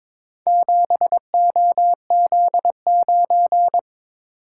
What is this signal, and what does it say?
Morse code, heard as a keyed tone, about 23 wpm, 700 Hz. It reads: 7OZ9
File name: practice / 22wpm / Four Letter-Number Combinations - 22wpm